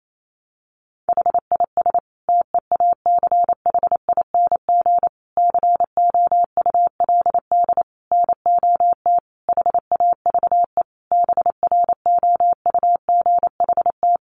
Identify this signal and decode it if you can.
Morse code, heard as a keyed tone, about 28 wpm, 700 Hz. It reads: HIS TEACHING COULD NOT HAVE BROUGHT
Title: Sentences from Top 1100 Words 28wpm